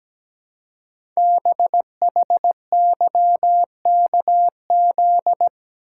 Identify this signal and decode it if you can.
Morse code, heard as a keyed tone, about 17 wpm, 700 Hz. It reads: BHYKZ
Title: Code Groups - 5 Letters - 17wpm